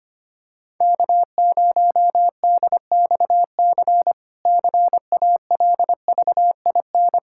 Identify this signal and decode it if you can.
Morse code, heard as a keyed tone, about 25 wpm, 700 Hz. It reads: K0DXC CALVIN
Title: K1USN SST QSO Practice - 25wpm